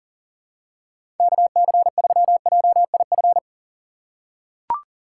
Morse code, heard as a keyed tone, about 40 wpm, 700 Hz. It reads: KC3JIF
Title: USA Call Signs - 6 Characters - 2x Character Spacing - 40wpm